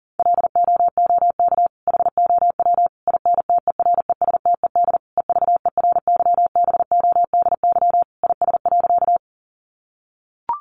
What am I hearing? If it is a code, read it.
Morse code, heard as a keyed tone, about 40 wpm, 700 Hz. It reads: LOOK HOW INTERESTED EVERYBODY IS.
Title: Sentences from Top 1400 Words 40wpm